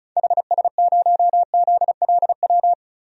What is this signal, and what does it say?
Morse code, heard as a keyed tone, about 36 wpm, 700 Hz. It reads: HS0ZLW
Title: DXCC-100 Call Signs - 6 Characters - Repeated 2x - 35wpm